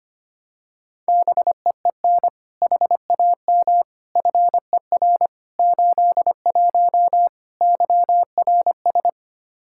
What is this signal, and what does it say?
Morse code, heard as a keyed tone, about 25 wpm, 700 Hz. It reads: BEEN HAM FER 81 YRS
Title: Practice QSO - Exchange 3 - More Info - 25wpm